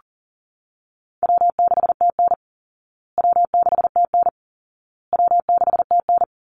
Morse code, heard as a keyed tone, about 40 wpm, 700 Hz. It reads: W6TN W6TN W6TN
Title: Easy Call Signs Repeated 3x - 40wpm